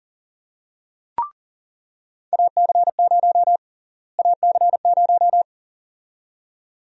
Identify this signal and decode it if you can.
Morse code, heard as a keyed tone, about 40 wpm, 700 Hz. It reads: AC0 AC0
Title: USA Call Sign Prefixes - Repeated 2x - 40wpm